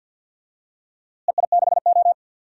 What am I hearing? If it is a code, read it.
Morse code, heard as a keyed tone, about 46 wpm, 700 Hz. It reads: EI6O
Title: DXCC-100 Call Signs - 4 Characters - Repeated 2x - 50wpm